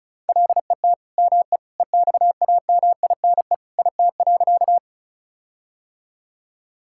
Morse code, heard as a keyed tone, about 36 wpm, 700 Hz. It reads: LET ME EXAMINE IT.
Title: Sentences from Top 1600 Words 35wpm